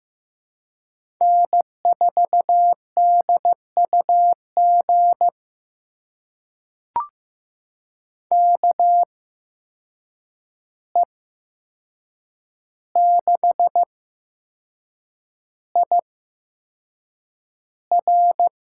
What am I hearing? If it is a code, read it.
Morse code, heard as a keyed tone, about 15 wpm, spaced as for 6 wpm, 700 Hz. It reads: N4DUG KE6IR
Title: USA Call Signs - 5 Characters - 8x Character Spacing - 15wpm